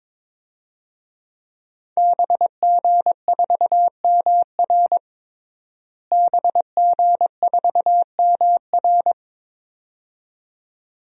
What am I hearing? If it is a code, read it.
Morse code, heard as a keyed tone, about 22 wpm, 700 Hz. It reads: BG4MR BG4MR
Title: DXCC-100 Call Signs - 5 Characters - Repeated 2x - 22wpm